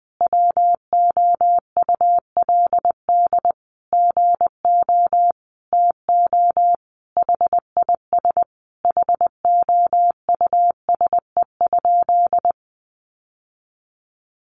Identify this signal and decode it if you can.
Morse code, heard as a keyed tone, about 20 wpm, 700 Hz. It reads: WOULD GO TO HIS HOUSE?
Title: Sentences from Top 600 Words 20wpm